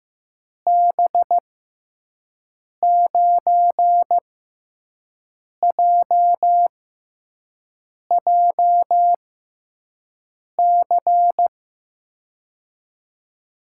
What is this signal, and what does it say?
Morse code, heard as a keyed tone, about 15 wpm, spaced as for 5 wpm, 700 Hz. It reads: B9JJC
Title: USA Call Signs - 6 Characters - 6x Character Spacing - 15wpm